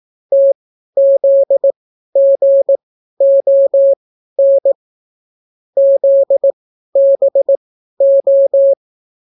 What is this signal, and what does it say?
Morse code, heard as a keyed tone, about 18 wpm, spaced as for 12 wpm, 550 Hz. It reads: TZGON ZBO